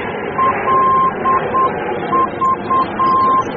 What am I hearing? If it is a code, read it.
Morse code, heard as a keyed tone, about 8 wpm, 1050 Hz, with noise. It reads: LV
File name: germany / mx / LV-D